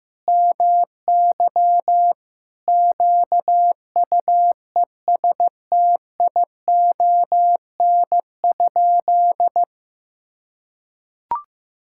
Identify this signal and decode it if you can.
Morse code, heard as a keyed tone, about 15 wpm, 700 Hz. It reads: MY QUESTION?
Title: Sentences from Top 1000 Words - Review Part 5 - 15wpm